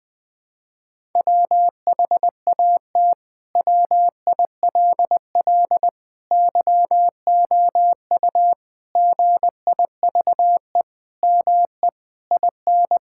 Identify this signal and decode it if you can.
Morse code, heard as a keyed tone, about 20 wpm, 700 Hz. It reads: WHAT WILL YOU GIVE ME IN
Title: Sentences from Top 1000 Words - Review Part 6 - 20wpm